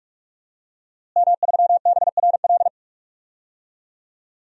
Morse code, heard as a keyed tone, about 46 wpm, 700 Hz. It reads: M3BRL
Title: DXCC-100 Call Signs - 5 Characters - Repeated 3x - 45wpm